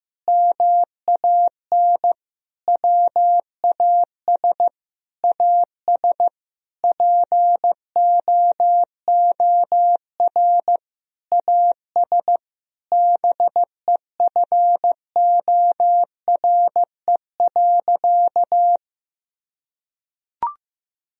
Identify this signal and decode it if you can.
Morse code, heard as a keyed tone, about 15 wpm, 700 Hz. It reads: MAN WAS AS POOR AS BEFORE.
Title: Sentences from Top 1000 Words - Review Part 2 - 15wpm